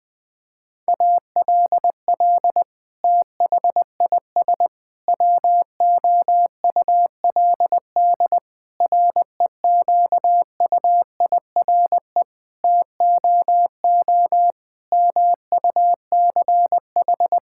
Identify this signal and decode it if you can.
Morse code, heard as a keyed tone, about 20 wpm, 700 Hz. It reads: ALL THIS WOULD REQUIRE TOO MUCH
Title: Sentences from Top 1000 Words - Review Part 6 - 20wpm